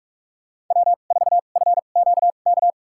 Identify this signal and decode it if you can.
Morse code, heard as a keyed tone, about 46 wpm, spaced as for 33 wpm, 700 Hz. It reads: W4FQK
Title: USA Call Signs - 5 Characters - 2x Character Spacing - 45wpm